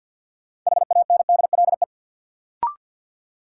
Morse code, heard as a keyed tone, about 47 wpm, 700 Hz. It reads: HANDLE
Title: Sentences from Top 1100 Words - New Word Drill - 50wpm